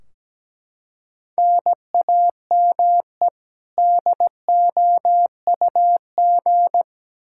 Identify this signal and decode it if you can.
Morse code, heard as a keyed tone, about 17 wpm, 700 Hz. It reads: NAME DOUG
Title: Practice QSO - Exchange 1 - Beginning - 17wpm